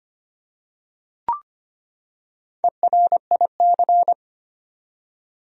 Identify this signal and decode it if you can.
Morse code, heard as a keyed tone, about 25 wpm, 700 Hz. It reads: ERIC
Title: USA Common Names - 25wpm